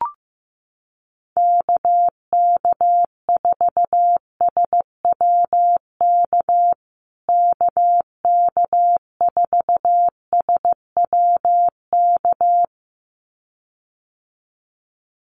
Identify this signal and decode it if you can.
Morse code, heard as a keyed tone, about 15 wpm, 700 Hz. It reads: KK4SWK KK4SWK
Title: USA Call Signs Repeated 2x 15wpm